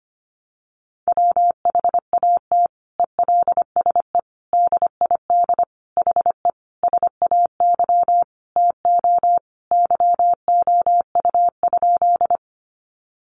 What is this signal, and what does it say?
Morse code, heard as a keyed tone, about 25 wpm, 700 Hz. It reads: WHAT ELSE DID HE SAY TO YOU?
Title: Sentences from Top 800 Words 25wpm